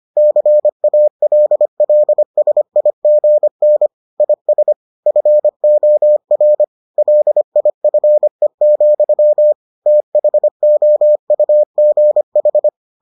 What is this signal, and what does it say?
Morse code, heard as a keyed tone, about 25 wpm, 600 Hz. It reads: CALLSIGN IS FOR LIFE, THOUGH